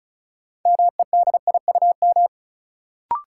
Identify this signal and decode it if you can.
Morse code, heard as a keyed tone, about 36 wpm, 700 Hz. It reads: MEDIUM
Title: Sentences from Top 900 Words - New Word Drill - 35wpm